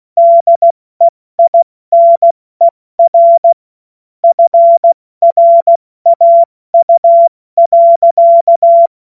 Morse code, heard as a keyed tone, about 16 wpm, 680 Hz. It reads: DEINER FRAU.
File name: Morsekurs / Witze / Witz 15. Februar 2022 70 F80